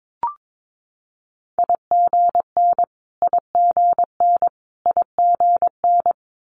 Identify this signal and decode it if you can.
Morse code, heard as a keyed tone, about 22 wpm, 700 Hz. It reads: IGN IGN IGN
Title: Multi-letter Phonemes Repeated 3x 22wpm